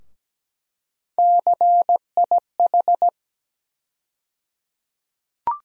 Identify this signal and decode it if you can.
Morse code, heard as a keyed tone, about 17 wpm, 700 Hz. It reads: CIH
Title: USA Call Sign Postfixes - 3 Letters 17wpm